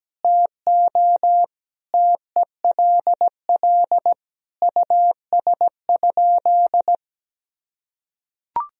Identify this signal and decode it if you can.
Morse code, heard as a keyed tone, about 17 wpm, 700 Hz. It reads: TO TELL US?
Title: Sentences from Top 1000 Words - Review Part 4 - 17wpm